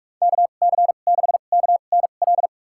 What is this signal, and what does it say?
Morse code, heard as a keyed tone, about 46 wpm, spaced as for 32 wpm, 700 Hz. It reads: KC6KNL
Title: USA Call Signs - 6 Characters - 2x Character Spacing - 45wpm